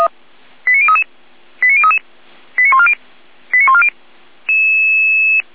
Five-tone selective call (ZVEI), heard as quick runs of five tones, tones of about 70 ms. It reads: 89033 89033 89140 89140
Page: SDR/ZVEI